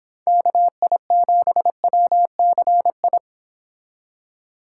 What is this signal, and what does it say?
Morse code, heard as a keyed tone, about 26 wpm, 700 Hz. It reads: KI7WCI
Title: Speed Racing - USA 2x3 Call Signs 22wpm